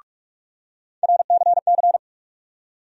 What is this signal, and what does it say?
Morse code, heard as a keyed tone, about 44 wpm, 700 Hz. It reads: RCC
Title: USA Call Sign Postfixes - 3 Letters 45wpm